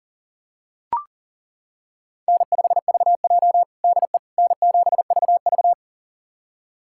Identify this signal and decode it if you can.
Morse code, heard as a keyed tone, about 40 wpm, 700 Hz. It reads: N5VJ DE N7VV
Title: USA Callsign DE Callsign - 4 Characters - 40wpm